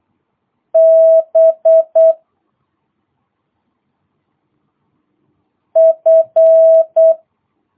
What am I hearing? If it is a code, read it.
Morse code, heard as a keyed tone, about 8 wpm, 650 Hz. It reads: B F